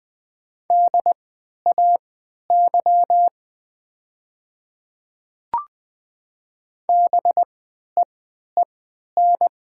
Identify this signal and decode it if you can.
Morse code, heard as a keyed tone, about 20 wpm, spaced as for 11 wpm, 700 Hz. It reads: DAY BEEN
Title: 3x Character Spacing - Top 200 Words - 20wpm